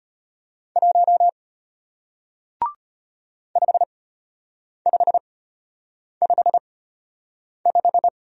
Morse code, heard as a keyed tone, about 34 wpm, spaced as for 8 wpm, 700 Hz. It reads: 1 5555